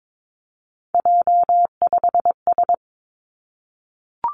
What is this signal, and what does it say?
Morse code, heard as a keyed tone, about 22 wpm, 700 Hz. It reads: J5S
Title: Three Challenging Letter-Number Combinations 22wpm